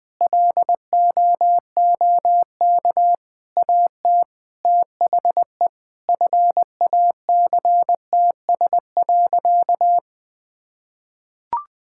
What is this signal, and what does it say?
Morse code, heard as a keyed tone, about 21 wpm, 700 Hz. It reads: LOOK AT THE FACTS.
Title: Sentences with 8-Word Phrases - Spoken Beginning - 20wpm